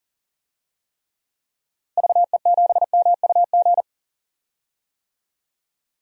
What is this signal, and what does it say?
Morse code, heard as a keyed tone, about 40 wpm, 700 Hz. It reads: VE7MUG
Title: DXCC-100 Call Signs - 6 Characters - Repeated 3x - 40wpm